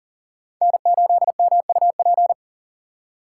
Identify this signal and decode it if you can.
Morse code, heard as a keyed tone, about 40 wpm, 700 Hz. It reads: N8MUP
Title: USA Call Signs - 5 Characters - 6x Character Spacing - 40wpm